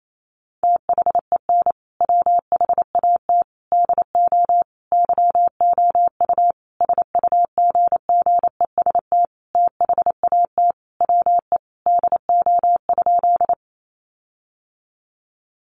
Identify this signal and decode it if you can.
Morse code, heard as a keyed tone, about 28 wpm, 700 Hz. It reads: THEN WHAT DO YOU SUGGEST THAT WE DO?
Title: Sentences from Top 1300 Words 28wpm